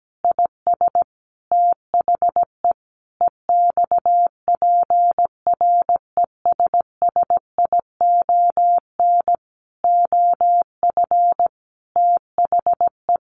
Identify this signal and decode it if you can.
Morse code, heard as a keyed tone, about 17 wpm, 700 Hz. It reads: IS THE EXPRESSION OF THE
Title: Sentences from Top 1500 Words 17wpm